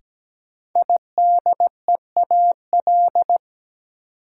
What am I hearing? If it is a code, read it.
Morse code, heard as a keyed tone, about 17 wpm, 700 Hz. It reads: IDEAL
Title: Sentences from Top 1400 Words - New Word Drill - 17wpm